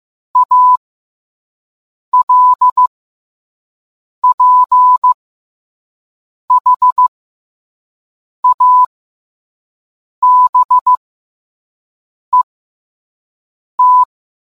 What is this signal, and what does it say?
Morse code, heard as a keyed tone, about 15 wpm, spaced as for 5 wpm, 1000 Hz. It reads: ALPHABET